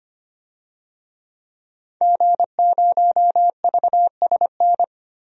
Morse code, heard as a keyed tone, about 25 wpm, 700 Hz. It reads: G0VSN
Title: DXCC-100 Call Signs - 5 Characters - 25wpm